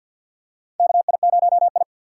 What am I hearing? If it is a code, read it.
Morse code, heard as a keyed tone, about 52 wpm, 700 Hz. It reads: KI0I
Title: K1USN SST QSO Practice - 50wpm